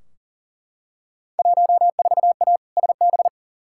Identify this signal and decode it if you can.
Morse code, heard as a keyed tone, about 40 wpm, 700 Hz. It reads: 14A SB
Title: Field Day Exchanges Encore - 40wpm